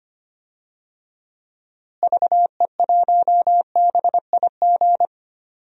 Morse code, heard as a keyed tone, about 25 wpm, 700 Hz. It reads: VE1BIG
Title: DXCC-100 Call Signs - 6 Characters - Repeated 3x - 25wpm